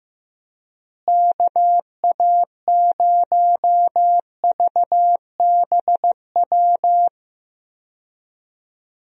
Morse code, heard as a keyed tone, about 15 wpm, 700 Hz. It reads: KA0VBW
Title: USA Call Signs - 6 Characters - Repeated 2x - 15wpm